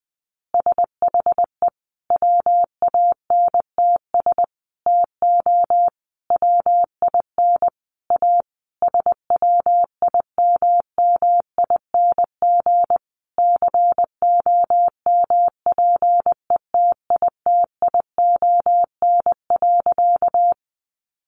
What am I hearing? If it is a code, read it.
Morse code, heard as a keyed tone, about 20 wpm, 700 Hz. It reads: SHE WANTS TO WIN A SWIMMING COMPETITION.
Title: Additional Easy Phrases 20wpm